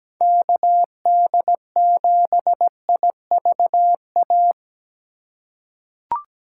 Morse code, heard as a keyed tone, about 17 wpm, 700 Hz. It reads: KD7IVA
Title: USA Call Signs - 2x Character Spacing - 17wpm